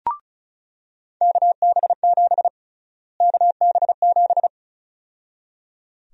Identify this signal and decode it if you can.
Morse code, heard as a keyed tone, about 36 wpm, 700 Hz. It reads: KB7 KB7
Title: USA Call Sign Prefixes - Repeated 2x - 35wpm